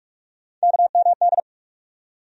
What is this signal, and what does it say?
Morse code, heard as a keyed tone, about 46 wpm, 700 Hz. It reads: KMD
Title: USA Call Sign Postfixes - 3 Letters 45wpm